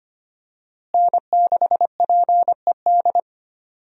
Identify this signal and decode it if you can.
Morse code, heard as a keyed tone, about 25 wpm, 700 Hz. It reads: N6PED
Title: USA Call Signs - 5 Characters - Repeated 2x - 25wpm